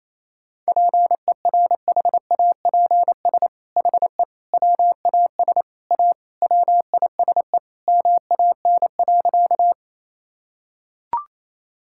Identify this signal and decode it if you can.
Morse code, heard as a keyed tone, about 28 wpm, 700 Hz. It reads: PERHAPS HE WAS A WISE MAN.
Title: Sentences from Top 1600 Words 28wpm